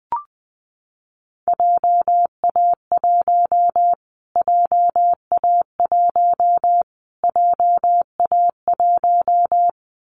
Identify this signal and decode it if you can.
Morse code, heard as a keyed tone, about 20 wpm, 700 Hz. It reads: JA1 JA1 JA1
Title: Callsign Prefixes for Top 25 DXCC - Repeated 3x - 20wpm